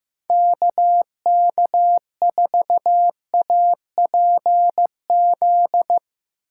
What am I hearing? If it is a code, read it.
Morse code, heard as a keyed tone, about 15 wpm, 700 Hz. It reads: KK4APZ
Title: Speed Racing - USA 2x3 Call Signs 15wpm